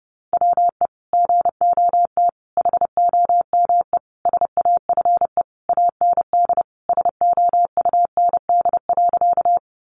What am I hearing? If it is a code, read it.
Morse code, heard as a keyed tone, about 30 wpm, 700 Hz. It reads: WE GOT HOME SAFE AND SOUND.
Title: Sentences from Top 900 Words 30wpm